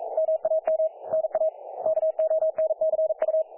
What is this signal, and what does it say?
Morse code, heard as a keyed tone, about 43 wpm, 640 Hz. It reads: MAM NA KONCU
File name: dlazaawansowanych / nagraniazpasma / websdr recording 3564.7kHz